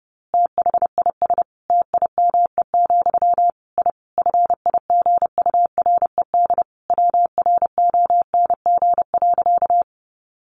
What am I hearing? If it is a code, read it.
Morse code, heard as a keyed tone, about 28 wpm, 700 Hz. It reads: THIS TIME, I FIGURED WRONG.
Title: A Sound Of Thunder 30wpm